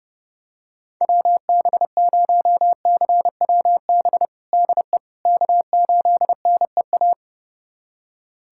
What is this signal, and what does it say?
Morse code, heard as a keyed tone, about 30 wpm, 700 Hz. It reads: WB0CWB DE K8NEA
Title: USA Callsign DE Callsign 30wpm